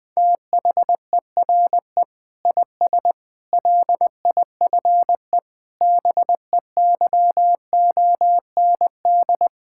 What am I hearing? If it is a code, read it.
Morse code, heard as a keyed tone, about 20 wpm, 700 Hz. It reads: THERE IS LIFE BEYOND
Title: Sentences from Top 1000 Words - Review Part 6 - 20wpm